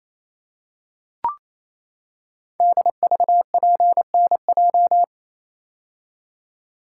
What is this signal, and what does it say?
Morse code, heard as a keyed tone, about 28 wpm, 700 Hz. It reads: DVPNJ
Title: Code Groups - 5 Letters - 28wpm